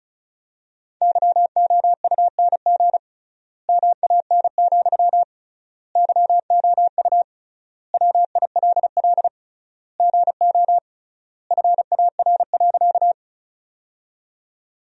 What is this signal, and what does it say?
Morse code, heard as a keyed tone, about 36 wpm, 700 Hz. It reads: YOUNG MAN, YOU WILL GO FAR.
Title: 3x Word-Spacing - Sentences from Top 400 Words - 35wpm